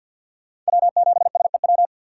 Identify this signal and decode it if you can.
Morse code, heard as a keyed tone, about 53 wpm, 700 Hz. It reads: W7SEW
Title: USA Call Signs - 5 Characters - 6x Character Spacing - 50wpm